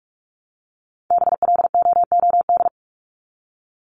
Morse code, heard as a keyed tone, about 44 wpm, 700 Hz. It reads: BLOOD
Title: Top 1001-1250 Words 45wpm